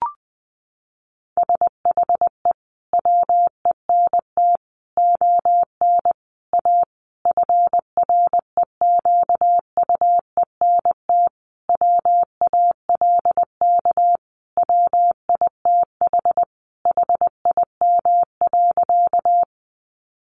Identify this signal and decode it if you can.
Morse code, heard as a keyed tone, about 20 wpm, 700 Hz. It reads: SHE WENT ON A FREQUENT WALK WITH HIM.
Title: Sentences from Top 1400 Words 20wpm